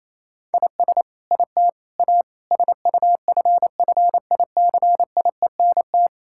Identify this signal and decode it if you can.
Morse code, heard as a keyed tone, about 28 wpm, 700 Hz. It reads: IS IT A SUFFICIENT
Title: Sentences from Top 1600 Words 28wpm